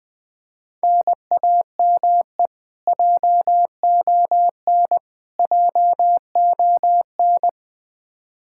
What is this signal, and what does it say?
Morse code, heard as a keyed tone, about 21 wpm, 700 Hz. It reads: NAME JON JON
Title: K1USN SST QSO Practice 20wpm